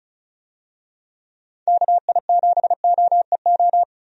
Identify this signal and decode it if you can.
Morse code, heard as a keyed tone, about 36 wpm, 700 Hz. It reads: KI7OEO
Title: USA Call Signs - 6 Characters - 35wpm